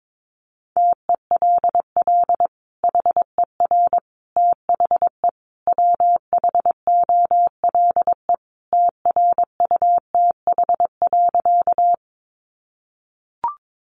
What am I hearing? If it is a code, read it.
Morse code, heard as a keyed tone, about 22 wpm, 700 Hz. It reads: TELL HER THE WHOLE TRUTH.
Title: Sentences from Top 800 Words 22wpm